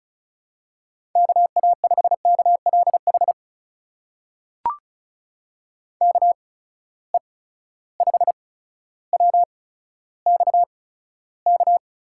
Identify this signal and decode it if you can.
Morse code, heard as a keyed tone, about 36 wpm, spaced as for 15 wpm, 700 Hz. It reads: KA5KLH KE5WXK
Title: USA Call Signs - 8x Character Spacing - 35wpm